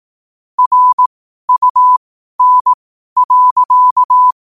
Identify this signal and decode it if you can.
Morse code, heard as a keyed tone, about 18 wpm, spaced as for 12 wpm, 1000 Hz. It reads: RUN.
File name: qotdfiles / 12wpm